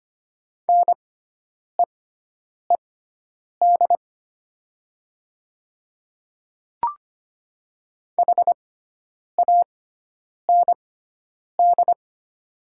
Morse code, heard as a keyed tone, about 25 wpm, spaced as for 9 wpm, 700 Hz. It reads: NEED HAND
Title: 6x Character Spacing - Top 300 Words - 25wpm